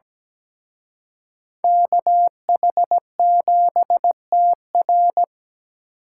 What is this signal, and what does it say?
Morse code, heard as a keyed tone, about 20 wpm, 700 Hz. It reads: KH7TR
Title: Speed Racing - USA 2x2 Call Signs 17wpm